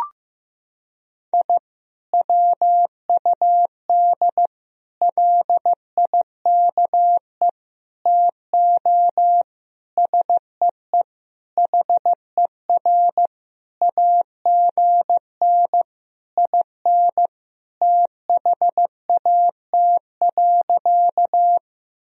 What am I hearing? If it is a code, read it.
Morse code, heard as a keyed tone, about 15 wpm, 700 Hz. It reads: I WUD LIKE TO SEE HER AGN IN THAT.
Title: Sentences from Top 200 Words and Abbreviations 15wpm